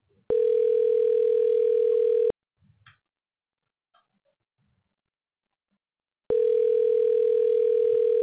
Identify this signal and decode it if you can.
Ringback tone (North America), 440 + 480 Hz.